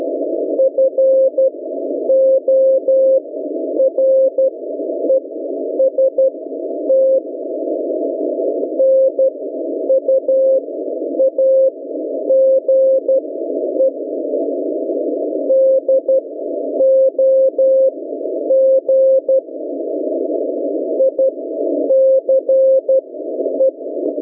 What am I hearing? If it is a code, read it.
Morse code, heard as a keyed tone, about 12 wpm, spaced as for 9 wpm, 530 Hz. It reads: FOREST NUAGE DOG ICE